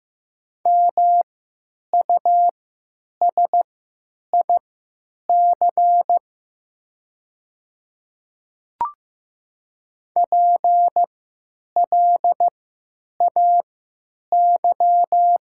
Morse code, heard as a keyed tone, about 15 wpm, spaced as for 9 wpm, 700 Hz. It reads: MUSIC PLAY
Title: 3x Character Spacing - Top 300 Words - 15wpm